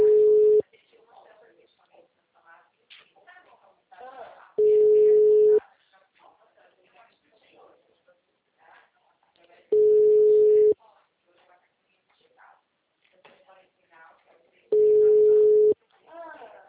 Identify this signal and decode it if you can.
Ringback tone (Europe, 425 Hz), with speech.
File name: out- -2002- - - .23420